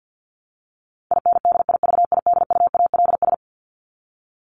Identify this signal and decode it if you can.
Morse code, heard as a keyed tone, about 50 wpm, 700 Hz. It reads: INDIVIDUALS
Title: ES:S Suffix - Top 100 Words with Base Words 50wpm